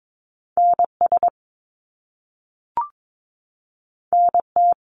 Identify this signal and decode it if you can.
Morse code, heard as a keyed tone, about 22 wpm, 700 Hz. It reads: NS NT
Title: Canadian Provinces and Territories 22wpm